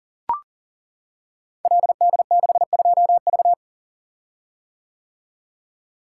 Morse code, heard as a keyed tone, about 42 wpm, 690 Hz. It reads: LD62V
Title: Code Groups - 5 Letter:Numbers - 40wpm